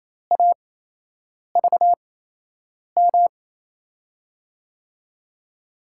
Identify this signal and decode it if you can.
Morse code, heard as a keyed tone, about 28 wpm, spaced as for 8 wpm, 700 Hz. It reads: AVM
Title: USA Call Signs - 5 Characters - 8x Character Spacing - 28wpm